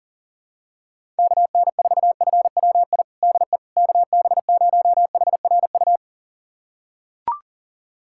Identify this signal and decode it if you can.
Morse code, heard as a keyed tone, about 38 wpm, 700 Hz. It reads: KN4FWI DE KB0HRU
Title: USA Callsign DE Callsign - 6 Characters - 40wpm